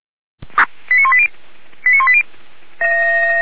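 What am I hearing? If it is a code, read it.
Five-tone selective call (ZVEI), heard as quick runs of five tones, tones of about 70 ms. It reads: 89180 89180
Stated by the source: ZVEI five-tone selective call received by SDR, recorded by the Feuerwehr Orlamünde fire brigade.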